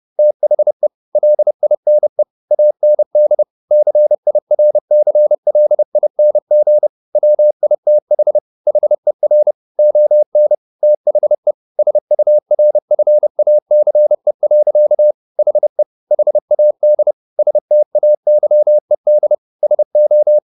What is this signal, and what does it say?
Morse code, heard as a keyed tone, about 30 wpm, 600 Hz. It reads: THE LINE AND CIRCLING WITH HER ON THE SURFACE. HE HAD STAYED SO
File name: the old man and the sea / the old man and the sea 1 30wpm